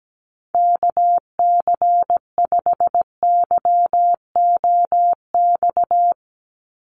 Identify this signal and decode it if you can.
Morse code, heard as a keyed tone, about 17 wpm, 700 Hz. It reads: KC5YOX
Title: USA Call Signs 17wpm